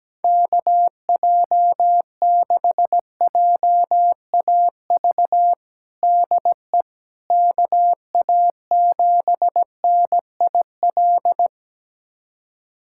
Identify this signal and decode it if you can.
Morse code, heard as a keyed tone, about 17 wpm, 700 Hz. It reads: KJ6JAV DE KA7NIL
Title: USA Callsign DE Callsign 17wpm